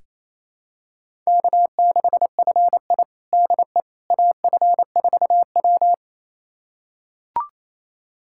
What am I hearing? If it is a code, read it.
Morse code, heard as a keyed tone, about 28 wpm, 700 Hz. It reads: K6FI DE AF4W
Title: USA Callsign DE Callsign - 4 Characters - 28wpm